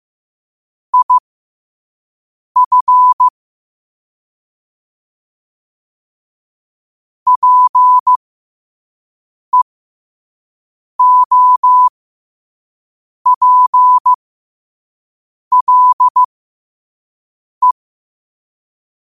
Morse code, heard as a keyed tone, about 15 wpm, spaced as for 5 wpm, 1000 Hz. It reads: IF PEOPLE
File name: qotdfiles / 5wpm